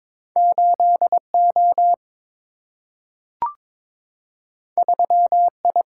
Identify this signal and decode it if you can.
Morse code, heard as a keyed tone, about 22 wpm, 700 Hz. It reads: 8O 3I
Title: Single-Letter Single-Number 22wpm